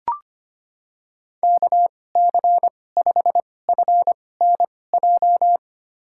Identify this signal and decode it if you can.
Morse code, heard as a keyed tone, about 25 wpm, spaced as for 18 wpm, 700 Hz. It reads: KC5FNJ
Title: USA Call Signs - 2x Character Spacing - 25wpm